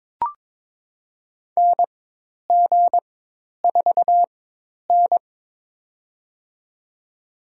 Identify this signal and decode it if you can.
Morse code, heard as a keyed tone, about 22 wpm, spaced as for 10 wpm, 700 Hz. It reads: NG4N
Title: USA Call Signs - 4 Characters - 4x Character Spacing - 22wpm